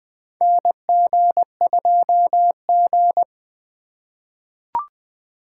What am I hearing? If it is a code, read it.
Morse code, heard as a keyed tone, about 20 wpm, 700 Hz. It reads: NG2G
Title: USA Call Signs - 4 Characters - 2x Character Spacing - 20wpm